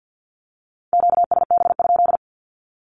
Morse code, heard as a keyed tone, about 48 wpm, 700 Hz. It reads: QSB?
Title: Q-Codes 50wpm